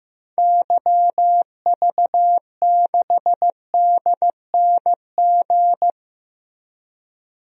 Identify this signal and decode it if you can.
Morse code, heard as a keyed tone, about 16 wpm, 700 Hz. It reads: YV6DNG
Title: DXCC-100 Call Signs - 6 Characters - Repeated 2x - 15wpm